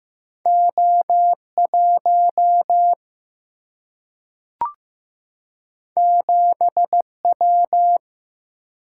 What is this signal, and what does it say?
Morse code, heard as a keyed tone, about 15 wpm, 700 Hz. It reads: O 1 7 W
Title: Single-Letter Single-Number 15wpm